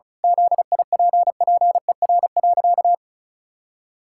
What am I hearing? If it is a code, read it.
Morse code, heard as a keyed tone, about 36 wpm, 700 Hz. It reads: ZIPPER.
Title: Pangram Challenge v2 - 35wpm